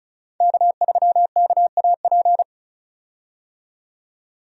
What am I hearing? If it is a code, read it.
Morse code, heard as a keyed tone, about 36 wpm, 700 Hz. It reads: K3KAP
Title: USA Call Signs - 5 Characters - 35wpm